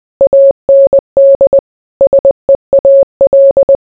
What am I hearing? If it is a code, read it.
Morse code, heard as a keyed tone, about 20 wpm, 550 Hz. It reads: AND SEAL